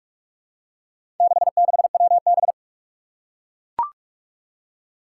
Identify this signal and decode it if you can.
Morse code, heard as a keyed tone, about 44 wpm, 700 Hz. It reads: 66WB
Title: Four Challenging Letter-Number Combinations 45wpm